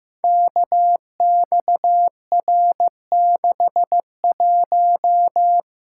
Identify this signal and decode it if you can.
Morse code, heard as a keyed tone, about 15 wpm, 700 Hz. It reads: KXR61
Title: Code Groups - 5 Letter:Numbers - 15wpm